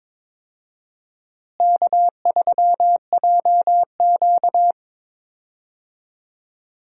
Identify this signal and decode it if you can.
Morse code, heard as a keyed tone, about 22 wpm, 700 Hz. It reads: K3JQ
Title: USA Call Signs - 4 Characters - Repeated 3x - 22wpm